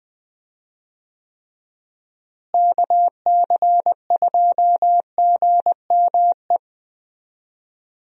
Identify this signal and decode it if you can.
Morse code, heard as a keyed tone, about 20 wpm, 700 Hz. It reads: KC2GME
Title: USA Call Signs - 6 Characters - Repeated 2x - 20wpm